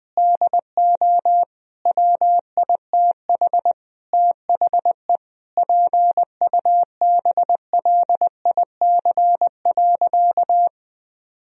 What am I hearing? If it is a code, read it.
Morse code, heard as a keyed tone, about 20 wpm, 700 Hz. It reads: DO WITH THE PUBLIC.
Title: Sentences from Top 1000 Words - Review Part 3 - 20wpm